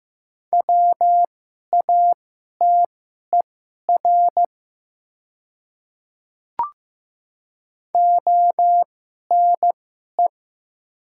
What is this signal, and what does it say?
Morse code, heard as a keyed tone, about 15 wpm, spaced as for 11 wpm, 690 Hz. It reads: WATER ONE
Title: 2x Character Spacing - Top 100 Words - 15wpm